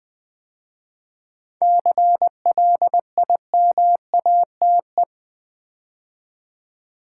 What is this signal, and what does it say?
Morse code, heard as a keyed tone, about 20 wpm, 700 Hz. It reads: CLIMATE
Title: 7 Letter Words - First 3 Letters Given - 20wpm